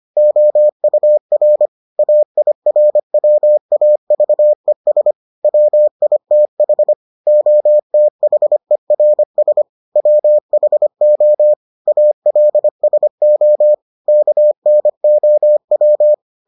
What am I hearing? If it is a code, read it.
Morse code, heard as a keyed tone, about 25 wpm, 600 Hz. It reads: OUR AIRWAVES WITH OTHERS WHO ALSO KNOW